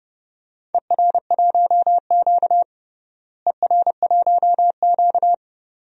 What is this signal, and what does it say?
Morse code, heard as a keyed tone, about 30 wpm, 700 Hz. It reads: ER1Q ER1Q
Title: DXCC-100 Call Signs - 4 Characters - Repeated 2x - 30wpm